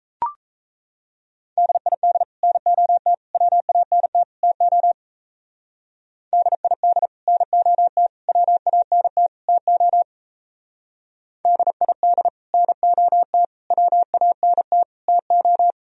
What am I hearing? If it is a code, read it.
Morse code, heard as a keyed tone, about 33 wpm, 700 Hz. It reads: DID NOT WANT TO DID NOT WANT TO DID NOT WANT TO
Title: Speed-Racing - Sets of 4 Words 28wpm